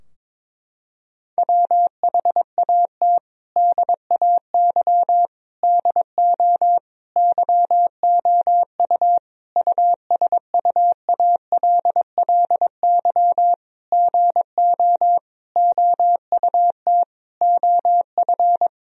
Morse code, heard as a keyed tone, about 22 wpm, 700 Hz. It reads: WHAT DAY DO YOU USUALLY GO OUT OF
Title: Sentences from Top 1000 Words - Review Part 3 - 22wpm